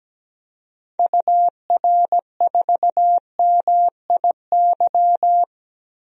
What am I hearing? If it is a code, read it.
Morse code, heard as a keyed tone, about 17 wpm, 700 Hz. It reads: UR4MIY
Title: DXCC-100 Call Signs - 6 Characters - 17wpm